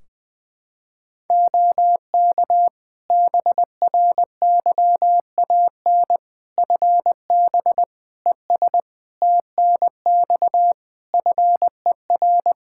Morse code, heard as a keyed tone, about 20 wpm, 700 Hz. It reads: OK BRYAN FB ES TNX FER
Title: Practice QSO - Exchange 2 - Info - 20wpm